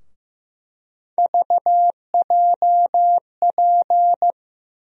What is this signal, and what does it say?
Morse code, heard as a keyed tone, about 15 wpm, 700 Hz. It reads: VJP
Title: USA Call Sign Postfixes - 3 Letters 15wpm